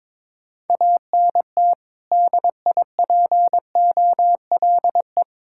Morse code, heard as a keyed tone, about 22 wpm, 700 Hz. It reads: ANT DIPOLE
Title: Practice QSO - Exchange 2 - Info - 22wpm